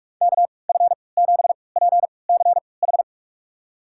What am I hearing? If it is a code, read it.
Morse code, heard as a keyed tone, about 47 wpm, spaced as for 26 wpm, 700 Hz. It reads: KF7PCH
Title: USA Call Signs - 6 Characters - 3x Character Spacing - 45wpm